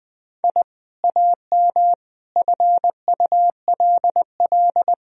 Morse code, heard as a keyed tone, about 20 wpm, 700 Hz. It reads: I AM FULL
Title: Sentences from Top 1000 Words - Review Part 3 - 20wpm